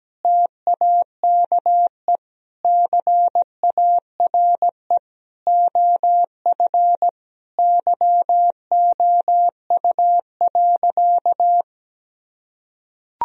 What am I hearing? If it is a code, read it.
Morse code, heard as a keyed tone, about 17 wpm, 700 Hz. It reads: TAKE CARE OF YOU.
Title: Sentences from Top 1000 Words - Review Part 1 - 17wpm